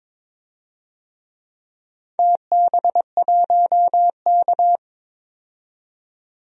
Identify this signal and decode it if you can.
Morse code, heard as a keyed tone, about 22 wpm, 700 Hz. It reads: TB1K
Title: DXCC-100 Call Signs - 4 Characters - 22wpm